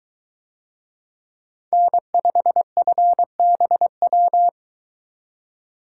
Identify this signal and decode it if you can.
Morse code, heard as a keyed tone, about 23 wpm, 700 Hz. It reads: N5FBW